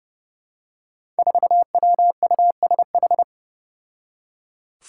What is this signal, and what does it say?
Morse code, heard as a keyed tone, about 30 wpm, 700 Hz. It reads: 4WUSH
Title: Five Challenging Letter-Number Combinations 30wpm